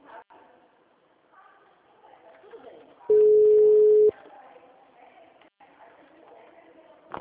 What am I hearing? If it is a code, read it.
Ringback tone (Europe, 425 Hz), with speech.
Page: Dracena (Ura-Reversa)